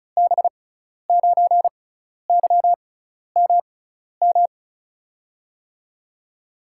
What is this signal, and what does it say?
Morse code, heard as a keyed tone, about 36 wpm, spaced as for 12 wpm, 700 Hz. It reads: B9YMM
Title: USA Call Signs - 6 Characters - 6x Character Spacing - 35wpm